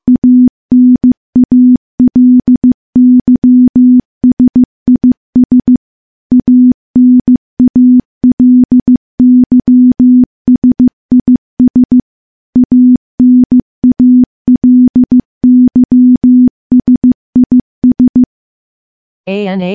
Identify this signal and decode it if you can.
Morse code, heard as a keyed tone, about 15 wpm, 250 Hz. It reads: ANALYSIS ANALYSIS ANALYSIS
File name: morse / morse-training-15WPM